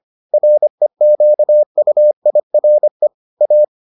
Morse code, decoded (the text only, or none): REQUIRE A